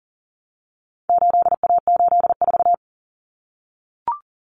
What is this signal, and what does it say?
Morse code, heard as a keyed tone, about 40 wpm, 700 Hz. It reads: 8A84
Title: Common Cut Numbers - 4-Digit 40wpm